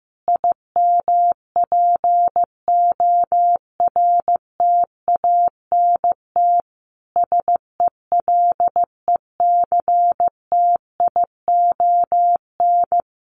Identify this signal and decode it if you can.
Morse code, heard as a keyed tone, about 15 wpm, 700 Hz. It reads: IMPORTANT SELECTION
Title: Sentences from Top 1400 Words 15wpm